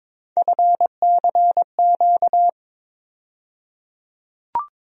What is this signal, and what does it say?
Morse code, heard as a keyed tone, about 22 wpm, 700 Hz. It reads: FCQ
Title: USA Call Sign Postfixes - 3 Letters 22wpm